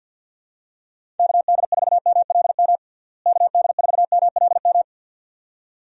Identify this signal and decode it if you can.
Morse code, heard as a keyed tone, about 46 wpm, 700 Hz. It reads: KD4MLM KD4MLM
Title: USA Call Signs - 6 Characters - Repeated 2x - 50wpm